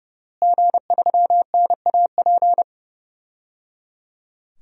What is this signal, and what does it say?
Morse code, heard as a keyed tone, about 30 wpm, 700 Hz. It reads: G3NAP
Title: DXCC-100 Call Signs - 5 Characters - Repeated 3x - 30wpm